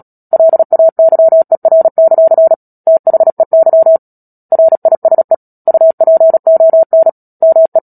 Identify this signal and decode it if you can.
Morse code, heard as a keyed tone, about 37 wpm, 650 Hz. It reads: LAYER; THEY RISE UPON ME
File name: Book Two / 10-Chapter 10 - Epilogue - 36 WPM